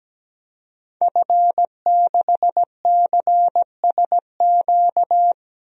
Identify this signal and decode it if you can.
Morse code, heard as a keyed tone, about 17 wpm, 700 Hz. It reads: F6CSQ
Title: DXCC-100 Call Signs - 5 Characters - 17wpm